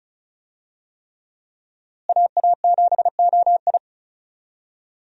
Morse code, heard as a keyed tone, about 34 wpm, 700 Hz. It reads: AA7OI
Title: USA Call Signs - 5 Characters - Repeated 3x - 35wpm